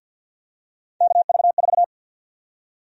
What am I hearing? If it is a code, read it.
Morse code, heard as a keyed tone, about 53 wpm, 700 Hz. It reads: KV4
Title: USA Call Sign Prefixes - Repeated 2x - 50wpm